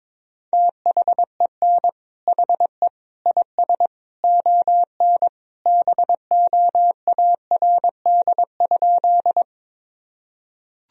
Morse code, heard as a keyed tone, about 22 wpm, 700 Hz. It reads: THEN HE IS ON BOARD?
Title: Sentences from Top 1000 Words - Review Part 4 - 22wpm